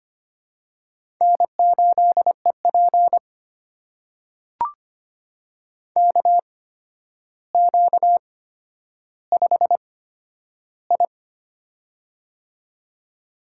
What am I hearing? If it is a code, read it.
Morse code, heard as a keyed tone, about 25 wpm, spaced as for 11 wpm, 700 Hz. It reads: N8EP KQ5I
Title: USA Call Signs - 4 Characters - 8x Character Spacing - 25wpm